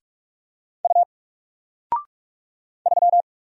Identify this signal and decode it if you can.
Morse code, heard as a keyed tone, about 46 wpm, 700 Hz. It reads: U 3
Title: Single Letter-Number 45wpm